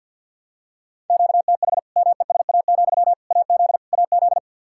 Morse code, heard as a keyed tone, about 47 wpm, 700 Hz. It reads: QTH MESA, AZ AZ
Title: Practice QSO - Exchange 1 - Beginning - 50wpm